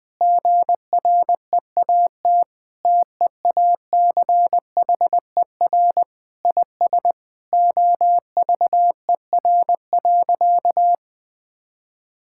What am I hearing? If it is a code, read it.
Morse code, heard as a keyed tone, about 20 wpm, 700 Hz. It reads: GREAT TEACHER IS OVER.
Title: Sentences from Top 1600 Words 20wpm